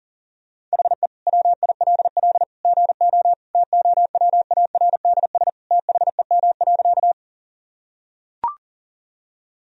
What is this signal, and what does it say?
Morse code, heard as a keyed tone, about 37 wpm, 700 Hz. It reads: HE WILL GO TOWARDS THEM.
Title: Sentences from Top 1000 Words - Review Part 2 - 40wpm